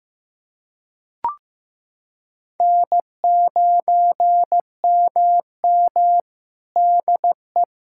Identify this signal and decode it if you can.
Morse code, heard as a keyed tone, about 15 wpm, 700 Hz. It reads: N9MM DE
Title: Practice QSO - Exchange 4 - Ending - 15wpm